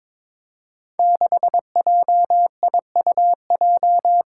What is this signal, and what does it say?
Morse code, heard as a keyed tone, about 22 wpm, 700 Hz. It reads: 6JIUJ
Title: Five Challenging Letter-Number Combinations 22wpm